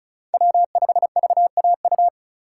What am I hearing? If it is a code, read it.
Morse code, heard as a keyed tone, about 36 wpm, 700 Hz. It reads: W5VAU
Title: USA Call Signs - 5 Characters - 4x Character Spacing - 35wpm